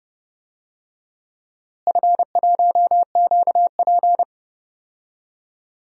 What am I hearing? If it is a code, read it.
Morse code, heard as a keyed tone, about 30 wpm, 700 Hz. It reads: F1QP
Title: DXCC-100 Call Signs - 4 Characters - Repeated 2x - 30wpm